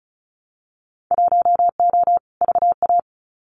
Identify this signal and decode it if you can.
Morse code, heard as a keyed tone, about 34 wpm, 700 Hz. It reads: 1O VA
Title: Winter Field Day Exchanges 35wpm